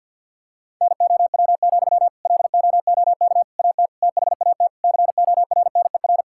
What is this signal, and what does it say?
Morse code, heard as a keyed tone, about 50 wpm, 700 Hz. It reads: NOW, LOOK AT THAT CORNER
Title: Sentences from Top 1600 Words 50wpm